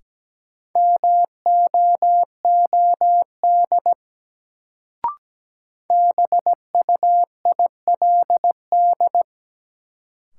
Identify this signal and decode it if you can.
Morse code, heard as a keyed tone, about 17 wpm, 700 Hz. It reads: MOOD BUILD
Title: Sentences from Top 1100 Words - New Word Drill Alt - 17wpm